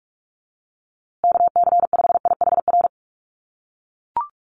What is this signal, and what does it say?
Morse code, heard as a keyed tone, about 44 wpm, 700 Hz. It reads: KC5IHR
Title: USA Call Signs Repeated 3x 45wpm